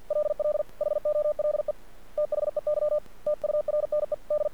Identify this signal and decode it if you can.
Morse code, heard as a keyed tone, about 50 wpm, 600 Hz. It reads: LL HOLE THEY TURNED
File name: archived / CRSnet